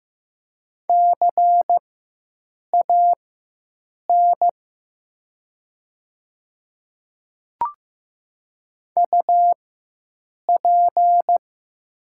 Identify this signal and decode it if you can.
Morse code, heard as a keyed tone, about 15 wpm, spaced as for 7 wpm, 700 Hz. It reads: CAN UP
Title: 4x Character Spacing - Top 100 Words - 15wpm